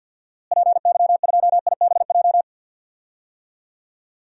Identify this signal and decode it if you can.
Morse code, heard as a keyed tone, about 51 wpm, 700 Hz. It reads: PY2IBJ
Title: DXCC-100 Call Signs - 6 Characters - Repeated 3x - 50wpm